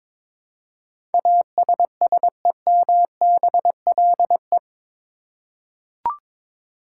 Morse code, heard as a keyed tone, about 22 wpm, 700 Hz. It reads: ASSEMBLE
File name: practice / 22wpm / Eight Letter Words - 22wpm